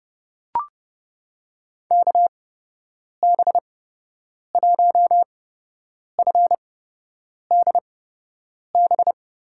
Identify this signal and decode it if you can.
Morse code, heard as a keyed tone, about 30 wpm, spaced as for 8 wpm, 700 Hz. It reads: KB1FDB
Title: USA Call Signs - 8x Character Spacing - 30wpm